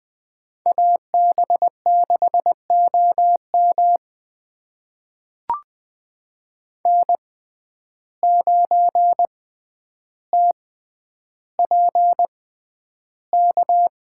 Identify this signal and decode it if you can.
Morse code, heard as a keyed tone, about 20 wpm, spaced as for 10 wpm, 700 Hz. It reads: AB6OM N9TPK
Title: USA Call Signs - 5 Characters - 6x Character Spacing - 20wpm